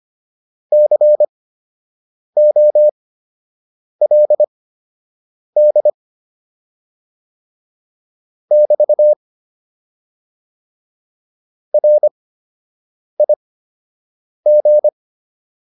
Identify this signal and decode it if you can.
Morse code, heard as a keyed tone, about 25 wpm, spaced as for 7 wpm, 600 Hz. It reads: COLD <BT> RIG